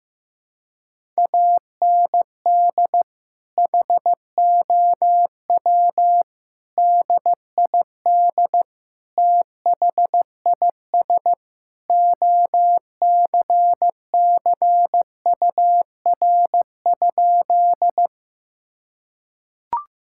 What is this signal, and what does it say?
Morse code, heard as a keyed tone, about 15 wpm, 700 Hz. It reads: AND HOW DID THIS OCCUR?
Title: Sentences from Top 1000 Words - Review Part 6 - 15wpm